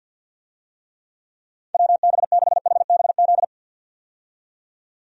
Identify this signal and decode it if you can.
Morse code, heard as a keyed tone, about 52 wpm, 700 Hz. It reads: WB6HBZ
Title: USA Call Signs - 6 Characters - 50wpm